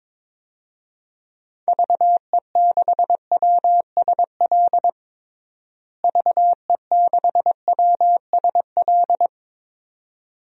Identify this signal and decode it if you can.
Morse code, heard as a keyed tone, about 22 wpm, 700 Hz. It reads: VE6WSL VE6WSL
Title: DXCC-100 Call Signs - 6 Characters - Repeated 2x - 22wpm